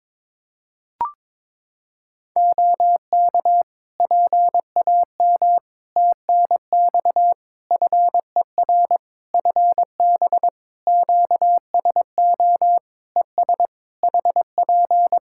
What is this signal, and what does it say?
Morse code, heard as a keyed tone, about 21 wpm, 700 Hz. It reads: OK PAM TNX FER FB QSO ES HP
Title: K1USN SST QSO Practice 22wpm